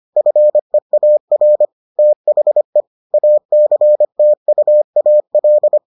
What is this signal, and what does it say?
Morse code, heard as a keyed tone, about 25 wpm, 600 Hz. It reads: FEAR THE ACTUAL